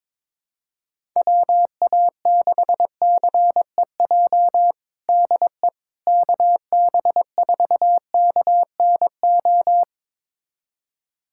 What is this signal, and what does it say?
Morse code, heard as a keyed tone, about 22 wpm, 700 Hz. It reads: WA6CEJ DE KB4KNO
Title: USA Callsign DE Callsign - 6 Characters - 22wpm